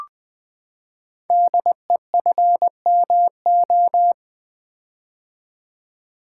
Morse code, heard as a keyed tone, about 20 wpm, 700 Hz. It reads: DEFMO
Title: Code Groups - 5 Letters - 20wpm